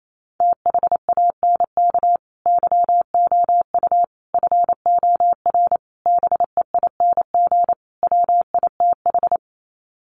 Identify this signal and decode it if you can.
Morse code, heard as a keyed tone, about 28 wpm, 700 Hz. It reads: THANK YOU FOR BEING WITH
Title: Sets of 5 Words 28wpm